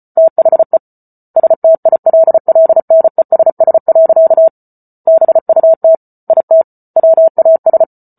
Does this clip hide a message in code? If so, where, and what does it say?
Yes: Morse code from 0.2 to 7.8 s, heard as a keyed tone, about 34 wpm, 650 Hz. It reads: THE STILLNESS. BUT IT WAS